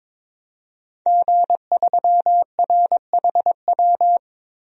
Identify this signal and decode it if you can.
Morse code, heard as a keyed tone, about 22 wpm, 700 Hz. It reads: G3RHW
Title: DXCC-100 Call Signs - 5 Characters - Repeated 3x - 22wpm